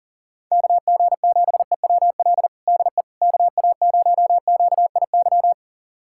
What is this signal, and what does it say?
Morse code, heard as a keyed tone, about 40 wpm, 700 Hz. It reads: KG7EWL DE KA0QIY
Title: USA Callsign DE Callsign - 6 Characters - 40wpm